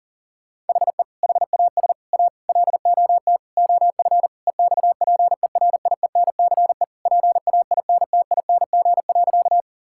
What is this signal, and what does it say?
Morse code, heard as a keyed tone, about 40 wpm, 700 Hz. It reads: HE HAS A LOT OF EXPERIENCE PAINTING.
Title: Sentences from Top 1000 Words - Review Part 6 - 40wpm